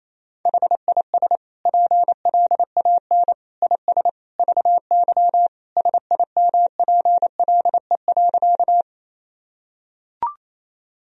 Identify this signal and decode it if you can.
Morse code, heard as a keyed tone, about 28 wpm, 700 Hz. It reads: HIS PLAN IS VY SIMPLE.
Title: Sentences from Top 500 Words and Abbreviations 28wpm